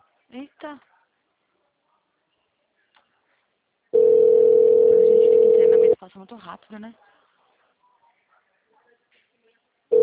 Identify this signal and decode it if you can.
Ringback tone (North America), 440 + 480 Hz, with speech.
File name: out- -2010- - - .28264